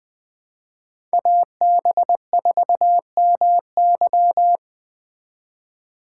Morse code, heard as a keyed tone, about 20 wpm, 700 Hz. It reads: AB4MY
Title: Speed Racing - USA 2x2 Call Signs 17wpm